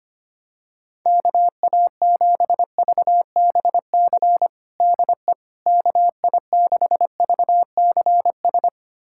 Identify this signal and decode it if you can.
Morse code, heard as a keyed tone, about 25 wpm, 700 Hz. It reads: KA7VBC DE KI6VCS
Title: USA Callsign DE Callsign - 6 Characters - 25wpm